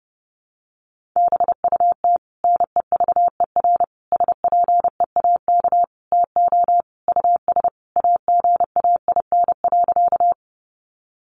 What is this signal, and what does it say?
Morse code, heard as a keyed tone, about 30 wpm, 700 Hz. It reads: BUT NEVER SPEAK TO US AGAIN.
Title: Sentences from Top 1100 Words 30wpm